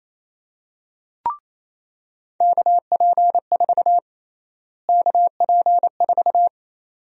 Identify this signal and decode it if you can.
Morse code, heard as a keyed tone, about 27 wpm, 700 Hz. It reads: KP4 KP4
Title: USA Call Sign Prefixes - Repeated 2x - 28wpm